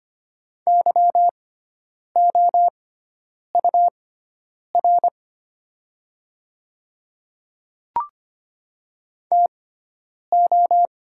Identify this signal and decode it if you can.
Morse code, heard as a keyed tone, about 25 wpm, spaced as for 9 wpm, 700 Hz. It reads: YOUR TO
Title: 6x Character Spacing - Top 100 Words - 25wpm